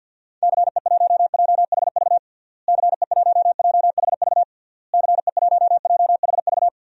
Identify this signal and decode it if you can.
Morse code, heard as a keyed tone, about 52 wpm, 700 Hz. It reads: CE1JHV CE1JHV CE1JHV
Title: DXCC-100 Call Signs - 6 Characters - Repeated 3x - 50wpm